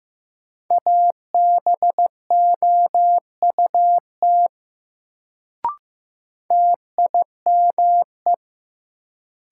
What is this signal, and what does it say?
Morse code, heard as a keyed tone, about 15 wpm, 700 Hz. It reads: ABOUT TIME
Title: Sentences from Top 100 Words - New Word Drill - 15wpm